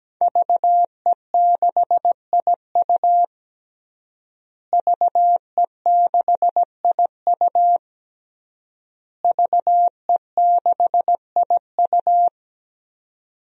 Morse code, heard as a keyed tone, about 17 wpm, 700 Hz. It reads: VE6IU VE6IU VE6IU
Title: DXCC-100 Call Signs - 5 Characters - Repeated 3x - 17wpm